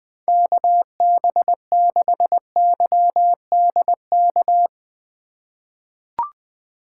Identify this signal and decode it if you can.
Morse code, heard as a keyed tone, about 20 wpm, 700 Hz. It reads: KB6YDK
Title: USA Call Signs - 6 Characters - 6x Character Spacing - 20wpm